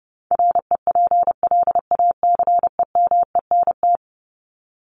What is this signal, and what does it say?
Morse code, heard as a keyed tone, about 30 wpm, 700 Hz. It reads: REPLACEMENT
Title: Top 1751-2000 Words 30wpm